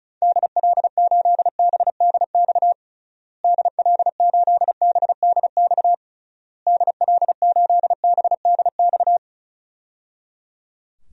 Morse code, heard as a keyed tone, about 36 wpm, 700 Hz. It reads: DL8BDX DL8BDX DL8BDX
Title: DXCC-100 Call Signs - 6 Characters - Repeated 3x - 35wpm